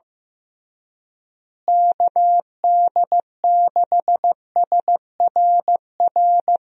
Morse code, heard as a keyed tone, about 15 wpm, 700 Hz. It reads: KD6SRR
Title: USA Call Signs - 6 Characters - Repeated 3x - 15wpm